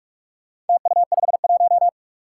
Morse code, heard as a keyed tone, about 46 wpm, 700 Hz. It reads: TU51
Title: Four Challenging Letter-Number Combinations 45wpm